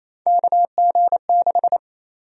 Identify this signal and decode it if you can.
Morse code, heard as a keyed tone, about 27 wpm, 700 Hz. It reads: KG6
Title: USA Call Sign Prefixes - 28wpm